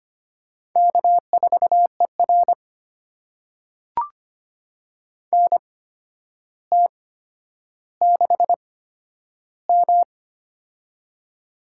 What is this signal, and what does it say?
Morse code, heard as a keyed tone, about 25 wpm, spaced as for 11 wpm, 700 Hz. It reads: K4ER NT6M
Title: USA Call Signs - 4 Characters - 8x Character Spacing - 25wpm